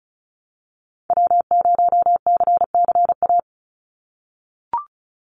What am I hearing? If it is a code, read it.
Morse code, heard as a keyed tone, about 36 wpm, 700 Hz. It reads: W0CCA
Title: NA SOTA Activator Callsigns 35wpm